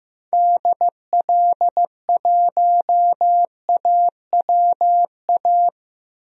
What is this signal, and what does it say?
Morse code, heard as a keyed tone, about 15 wpm, 700 Hz. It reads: DL1AWA
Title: DXCC-100 Call Signs - 6 Characters - Repeated 3x - 15wpm